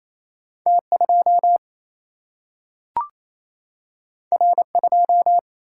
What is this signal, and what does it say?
Morse code, heard as a keyed tone, about 28 wpm, 700 Hz. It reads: T 2 R 2
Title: Single-Letter Single-Number 28wpm